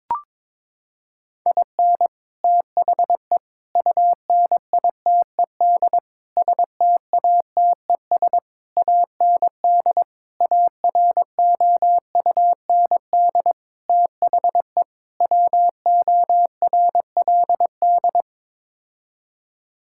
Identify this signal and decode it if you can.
Morse code, heard as a keyed tone, about 22 wpm, 700 Hz. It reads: IN THE UNITED STATES AND AROUND THE WORLD
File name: practice / 22wpm / Sets of 8 Words - 22wpm